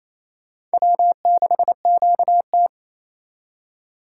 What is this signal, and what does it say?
Morse code, heard as a keyed tone, about 32 wpm, 700 Hz. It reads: W6QT
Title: Speed Racing - USA Call Signs 28wpm